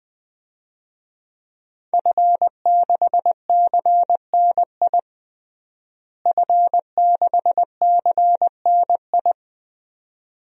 Morse code, heard as a keyed tone, about 20 wpm, 700 Hz. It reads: F6CNI F6CNI
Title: DXCC-100 Call Signs - 5 Characters - Repeated 2x - 20wpm